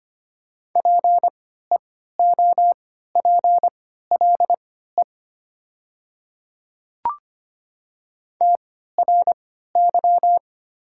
Morse code, heard as a keyed tone, about 25 wpm, spaced as for 14 wpm, 700 Hz. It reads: PEOPLE TRY
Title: 3x Character Spacing - Top 200 Words - 25wpm